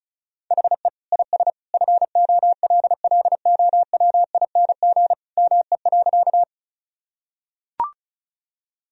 Morse code, heard as a keyed tone, about 36 wpm, 700 Hz. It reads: HE IS FOLLOWING ME.
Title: Sentences from Top 1000 Words - Review Part 4 - 35wpm